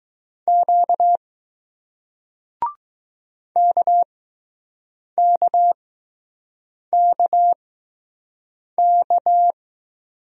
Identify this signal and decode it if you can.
Morse code, heard as a keyed tone, about 20 wpm, spaced as for 6 wpm, 700 Hz. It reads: Q KKKK